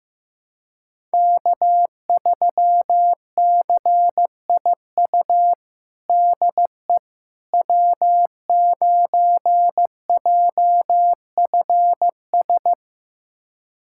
Morse code, heard as a keyed tone, about 15 wpm, 700 Hz. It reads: K3CIU DE W9JFS
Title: USA Callsign DE Callsign - 5 Characters - 15wpm